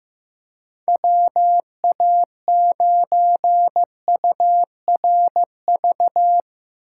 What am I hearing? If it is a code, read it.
Morse code, heard as a keyed tone, about 16 wpm, 700 Hz. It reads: WA9URV
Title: Speed Racing - USA 2x3 Call Signs 15wpm